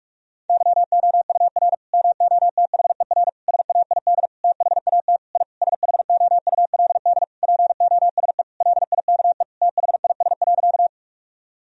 Morse code, heard as a keyed tone, about 46 wpm, 700 Hz. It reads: YOUR MOTHER SAID THAT I SHOULD POSE LIKE THIS.
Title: Sentences from Top 1000 Words - Review Part 2 - 45wpm